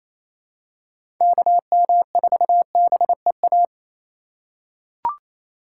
Morse code, heard as a keyed tone, about 28 wpm, 700 Hz. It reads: KM4BEA
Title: USA Call Signs - 6 Characters - 8x Character Spacing - 28wpm